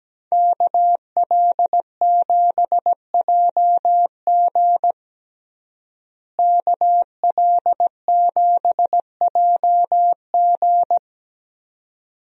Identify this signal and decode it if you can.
Morse code, heard as a keyed tone, about 17 wpm, 700 Hz. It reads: KL7JG KL7JG
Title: DXCC-100 Call Signs - 5 Characters - Repeated 2x - 17wpm